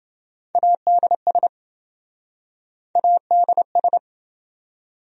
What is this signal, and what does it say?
Morse code, heard as a keyed tone, about 26 wpm, 700 Hz. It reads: ADS ADS